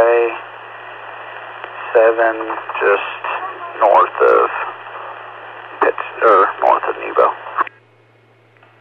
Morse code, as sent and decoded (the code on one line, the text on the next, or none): none